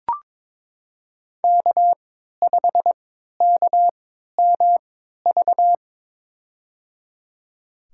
Morse code, heard as a keyed tone, about 22 wpm, spaced as for 13 wpm, 700 Hz. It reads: K5KMV
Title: USA Call Signs - 5 Characters - 3x Character Spacing - 22wpm